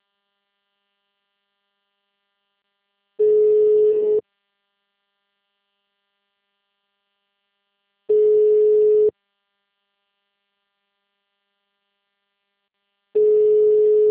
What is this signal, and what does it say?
Ringback tone (Europe, 425 Hz).
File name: out- -2014- - - .15091